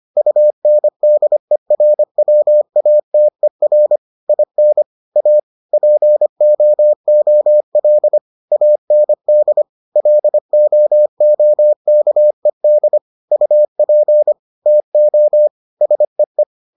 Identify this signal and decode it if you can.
Morse code, heard as a keyed tone, about 25 wpm, 600 Hz. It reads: UNDERWATER IN A POOL AND LOOKED UP TO SEE